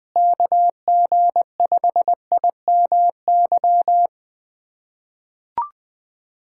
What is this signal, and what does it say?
Morse code, heard as a keyed tone, about 21 wpm, 700 Hz. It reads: KG5IMY
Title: USA Callsign DE Callsign - 6 Characters - 20wpm